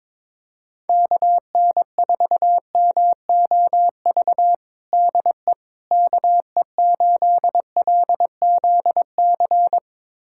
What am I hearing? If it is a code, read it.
Morse code, heard as a keyed tone, about 21 wpm, 700 Hz. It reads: KN4MOV DE KE8LZC
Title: USA Callsign DE Callsign - 6 Characters - 22wpm